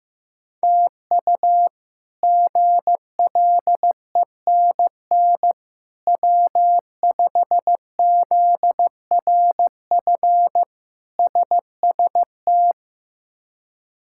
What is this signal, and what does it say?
Morse code, heard as a keyed tone, about 15 wpm, 700 Hz. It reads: TU GLENN W5ZRF SST
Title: K1USN SST QSO Practice - 15wpm